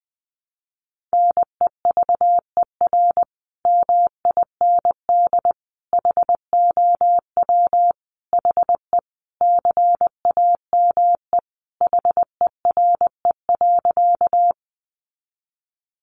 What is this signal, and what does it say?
Morse code, heard as a keyed tone, about 20 wpm, 700 Hz. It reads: NEVER MIND HOW HE CAME HERE.
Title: Sentences from Top 500 Words 20wpm